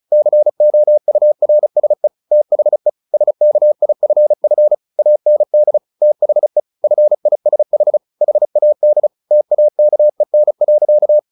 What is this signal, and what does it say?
Morse code, heard as a keyed tone, about 34 wpm, 600 Hz. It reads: COURSE THE SKIFF AND THE FISH HAD TAKEN.
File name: the old man and the sea 2 35wpm